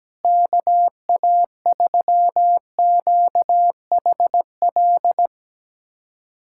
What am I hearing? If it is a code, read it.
Morse code, heard as a keyed tone, about 17 wpm, 700 Hz. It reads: KA3QHL
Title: USA Callsign DE Callsign 17wpm